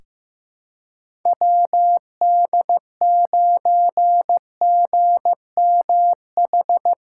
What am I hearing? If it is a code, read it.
Morse code, heard as a keyed tone, about 15 wpm, 700 Hz. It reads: WD9GMH
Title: USA Callsign DE Callsign 15wpm